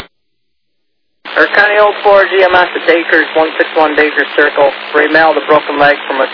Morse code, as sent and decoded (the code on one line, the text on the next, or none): none